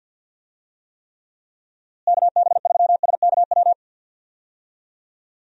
Morse code, heard as a keyed tone, about 45 wpm, 700 Hz. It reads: KB3SKW
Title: USA Call Signs - 6 Characters - Repeated 3x - 50wpm